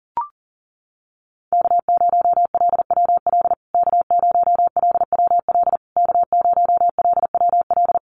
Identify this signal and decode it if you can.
Morse code, heard as a keyed tone, about 40 wpm, 700 Hz. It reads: K0LWL K0LWL K0LWL
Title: USA Call Signs Repeated 3x 40wpm